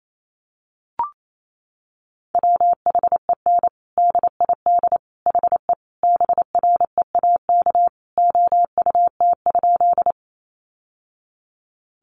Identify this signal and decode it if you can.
Morse code, heard as a keyed tone, about 28 wpm, 700 Hz. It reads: WHEN DID HE BREAK OUT?
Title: Sentences from Top 900 Words 28wpm